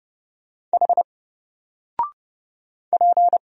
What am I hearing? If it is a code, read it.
Morse code, heard as a keyed tone, about 30 wpm, 700 Hz. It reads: H P
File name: Single-Letter-Number-30wpm